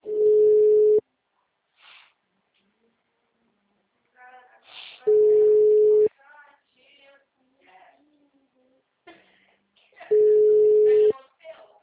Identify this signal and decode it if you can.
Ringback tone (Europe, 425 Hz), with speech.